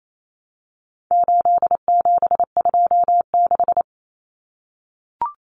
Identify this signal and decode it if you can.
Morse code, heard as a keyed tone, about 28 wpm, 700 Hz. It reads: 8726
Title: Common Cut Numbers - 4-Digit 28wpm